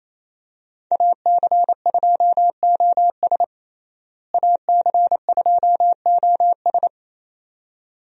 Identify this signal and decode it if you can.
Morse code, heard as a keyed tone, about 28 wpm, 700 Hz. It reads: AC2OS AC2OS
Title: USA Call Signs - 5 Characters - Repeated 2x - 28wpm